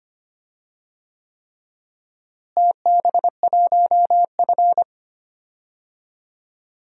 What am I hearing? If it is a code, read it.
Morse code, heard as a keyed tone, about 25 wpm, 700 Hz. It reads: TB1F
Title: DXCC-100 Call Signs - 4 Characters - Repeated 3x - 25wpm